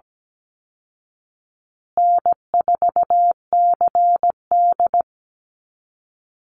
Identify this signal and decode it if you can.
Morse code, heard as a keyed tone, about 17 wpm, 700 Hz. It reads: N4CD
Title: Call Signs Repeated 2x - 17wpm